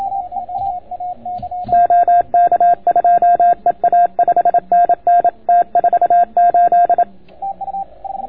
Morse code, heard as a keyed tone, about 27 wpm, 690 Hz. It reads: OK2EA 5NN T48